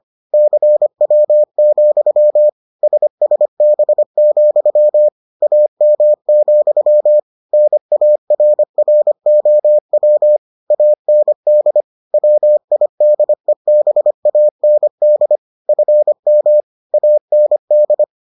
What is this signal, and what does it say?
Morse code, heard as a keyed tone, about 25 wpm, 600 Hz. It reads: CW, SSB, AM, NARROW AND WIDEBAND FM AND